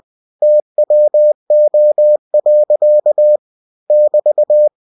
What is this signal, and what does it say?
Morse code, heard as a keyed tone, about 20 wpm, 600 Hz. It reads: TWO. <BT>
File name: cw-books / winnie-the-pooh / winnie-20wpm-page-0088